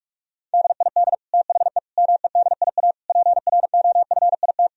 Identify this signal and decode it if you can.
Morse code, heard as a keyed tone, about 41 wpm, 700 Hz. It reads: DID THE MEDIA PROFIT
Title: Sentences from Top 1000 Words - Review Part 4 - 45wpm